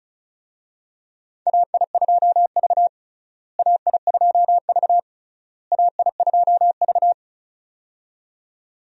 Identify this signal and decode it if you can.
Morse code, heard as a keyed tone, about 34 wpm, 700 Hz. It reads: AI2V AI2V AI2V
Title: USA Call Signs - 4 Characters - Repeated 3x - 35wpm